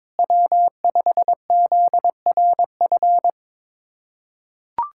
Morse code, heard as a keyed tone, about 22 wpm, 700 Hz. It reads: W5ZRF
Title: K1USN SST QSO Practice - 22wpm